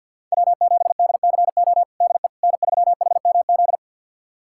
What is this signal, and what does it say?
Morse code, heard as a keyed tone, about 52 wpm, 700 Hz. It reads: W7DCO DE N3HMZ
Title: USA Callsign DE Callsign - 5 Characters - 50wpm